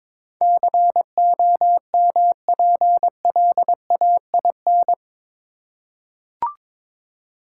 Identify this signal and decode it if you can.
Morse code, heard as a keyed tone, about 22 wpm, 700 Hz. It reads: COMPLAIN
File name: practice / 22wpm / Eight Letter Words - 22wpm